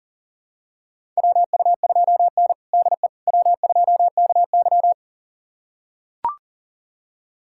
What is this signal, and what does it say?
Morse code, heard as a keyed tone, about 40 wpm, 700 Hz. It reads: WU2N DE W2KY
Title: USA Callsign DE Callsign - 4 Characters - 40wpm